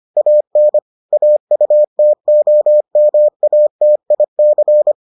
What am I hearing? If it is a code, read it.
Morse code, heard as a keyed tone, about 25 wpm, 600 Hz. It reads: AN AUTOMATIC